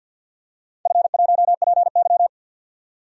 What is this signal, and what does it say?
Morse code, heard as a keyed tone, about 51 wpm, 700 Hz. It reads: F1PY
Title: DXCC-100 Call Signs - 4 Characters - 50wpm